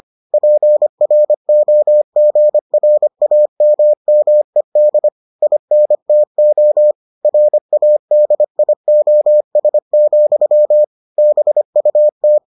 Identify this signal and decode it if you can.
Morse code, heard as a keyed tone, about 25 wpm, 600 Hz. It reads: PROGRAMMED INTO RADIOS, BUT